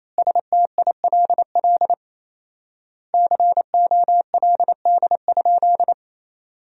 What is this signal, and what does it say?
Morse code, heard as a keyed tone, about 28 wpm, 700 Hz. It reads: STILL COLD?
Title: 4x Word-Spacing - Sentences from Top 400 Words - 28wpm